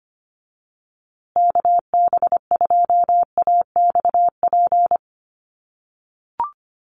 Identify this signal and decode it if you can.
Morse code, heard as a keyed tone, about 25 wpm, 700 Hz. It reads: KB2AXP
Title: USA Call Signs Repeated 2x 25wpm